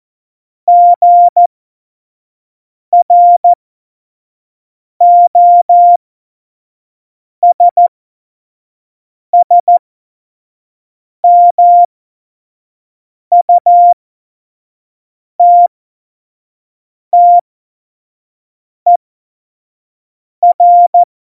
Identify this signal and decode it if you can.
Morse code, heard as a keyed tone, about 14 wpm, spaced as for 5 wpm, 700 Hz. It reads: GROSSMUTTER
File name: Morsekurs / Witze / Witz 09. Januar 2018, 25 F70